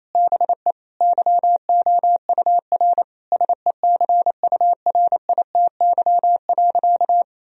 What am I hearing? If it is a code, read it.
Morse code, heard as a keyed tone, about 28 wpm, 700 Hz. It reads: BE YOUR SECURITY.
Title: Sentences from Top 1000 Words - Review Part 4 - 28wpm